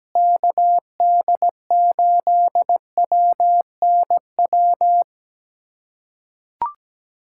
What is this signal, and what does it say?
Morse code, heard as a keyed tone, about 17 wpm, 700 Hz. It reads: KD8WNW
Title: USA Callsign DE Callsign - 6 Characters - 17wpm